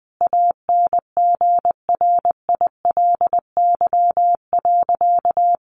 Morse code, heard as a keyed tone, about 20 wpm, 700 Hz. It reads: ANGRILY.
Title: A Sound Of Thunder 20wpm